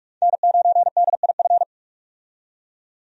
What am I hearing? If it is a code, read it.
Morse code, heard as a keyed tone, about 41 wpm, 700 Hz. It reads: N9DIF
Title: USA Call Signs - 5 Characters - 45wpm